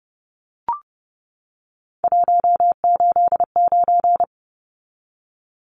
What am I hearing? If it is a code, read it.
Morse code, heard as a keyed tone, about 30 wpm, 700 Hz. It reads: 189
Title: Numbers - 3-Digit 30wpm